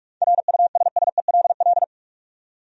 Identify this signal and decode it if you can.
Morse code, heard as a keyed tone, about 46 wpm, 700 Hz. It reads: RUSSELL
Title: USA Common Names - 45wpm